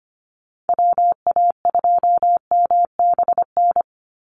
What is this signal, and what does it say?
Morse code, heard as a keyed tone, about 25 wpm, 700 Hz. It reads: WA2MBN
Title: USA Call Signs 25wpm